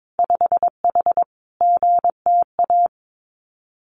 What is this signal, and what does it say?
Morse code, heard as a keyed tone, about 22 wpm, 700 Hz. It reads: 5H GTA
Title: Winter Field Day Exchanges Encore 22wpm